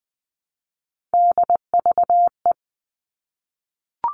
Morse code, heard as a keyed tone, about 20 wpm, 700 Hz. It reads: DVE
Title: All Cut Numbers - 3-Digit 20wpm